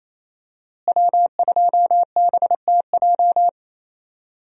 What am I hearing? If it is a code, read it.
Morse code, heard as a keyed tone, about 28 wpm, 700 Hz. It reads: W2BTJ
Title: USA Call Signs - 5 Characters - Repeated 2x - 28wpm